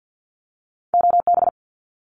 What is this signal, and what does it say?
Morse code, heard as a keyed tone, about 50 wpm, 700 Hz. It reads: OB
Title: Abbreviations 50wpm